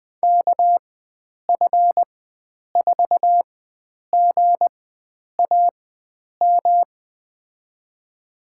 Morse code, heard as a keyed tone, about 20 wpm, spaced as for 9 wpm, 700 Hz. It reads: KF4GAM
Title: USA Call Signs - 6 Characters - 4x Character Spacing - 20wpm